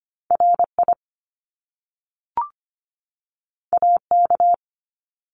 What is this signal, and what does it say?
Morse code, heard as a keyed tone, about 25 wpm, 700 Hz. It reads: RI AK
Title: US States 25wpm